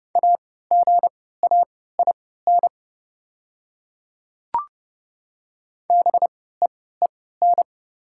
Morse code, heard as a keyed tone, about 30 wpm, spaced as for 17 wpm, 690 Hz. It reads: AGAIN BEEN
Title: 3x Character Spacing - Top 200 Words - 30wpm